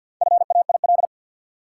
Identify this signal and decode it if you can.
Morse code, heard as a keyed tone, about 50 wpm, 700 Hz. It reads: FAIL
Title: Four Letter Words - 50wpm